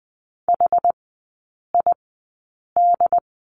Morse code, heard as a keyed tone, about 20 wpm, spaced as for 8 wpm, 700 Hz. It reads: HID
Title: 081 Lesson 14e - Number 2 - Three Characters 20wpm